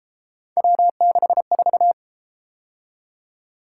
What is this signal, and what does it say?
Morse code, heard as a keyed tone, about 32 wpm, 700 Hz. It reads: W64